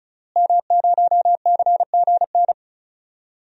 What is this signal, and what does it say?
Morse code, heard as a keyed tone, about 36 wpm, 700 Hz. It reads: M0CGN
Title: DXCC-100 Call Signs - 5 Characters - Repeated 3x - 35wpm